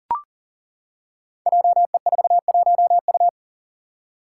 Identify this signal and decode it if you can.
Morse code, heard as a keyed tone, about 39 wpm, 690 Hz. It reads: JE41U
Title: Five Challenging Letter-Number Combinations 40wpm